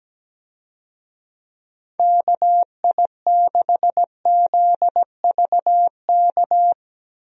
Morse code, heard as a keyed tone, about 17 wpm, 700 Hz. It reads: KI6ZVK
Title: USA Call Signs - 6 Characters - Repeated 3x - 17wpm